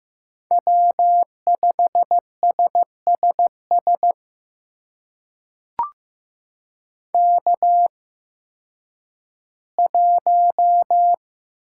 Morse code, heard as a keyed tone, about 15 wpm, spaced as for 10 wpm, 700 Hz. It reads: W5SSS K1
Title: USA Call Signs - 5 Characters - 8x Character Spacing - 15wpm